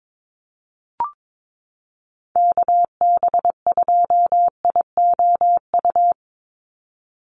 Morse code, heard as a keyed tone, about 22 wpm, 700 Hz. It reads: KB2IOU
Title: USA Call Signs 22wpm